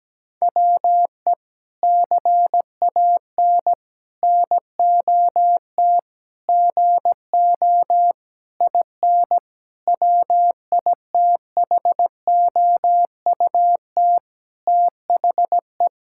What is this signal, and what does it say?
Morse code, heard as a keyed tone, about 17 wpm, 700 Hz. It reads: WE CAN NOT GO IN WITHOUT THE
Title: Sentences from Top 1000 Words - Review Part 6 - 17wpm